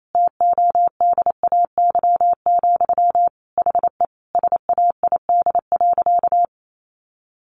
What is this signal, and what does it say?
Morse code, heard as a keyed tone, about 28 wpm, 700 Hz. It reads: TODAY, HE SAID.
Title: Sentences from Top 600 Words 28wpm